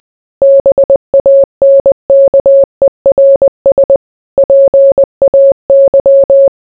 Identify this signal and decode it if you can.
Morse code, heard as a keyed tone, about 20 wpm, 550 Hz. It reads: BANKERS PAY